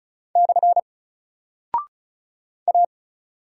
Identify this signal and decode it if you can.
Morse code, heard as a keyed tone, about 34 wpm, 700 Hz. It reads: / A
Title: Single Character (Letter:Number:Punctuation:Prosign) 35wpm